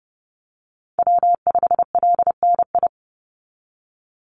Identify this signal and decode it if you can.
Morse code, heard as a keyed tone, about 30 wpm, 700 Hz. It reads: W5LNI
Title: Call Signs - 30wpm